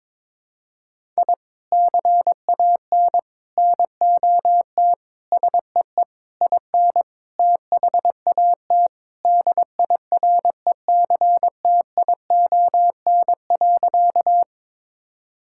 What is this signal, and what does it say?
Morse code, heard as a keyed tone, about 22 wpm, 700 Hz. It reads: I CAN NOT SEE IN THAT DIRECTION.
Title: Sentences from Top 1000 Words - Review Part 5 - 22wpm